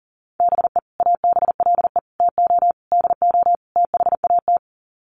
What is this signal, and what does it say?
Morse code, heard as a keyed tone, about 40 wpm, 700 Hz. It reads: BE ABLE TO DO THAT
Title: Sets of 5 Words Encore2 40wpm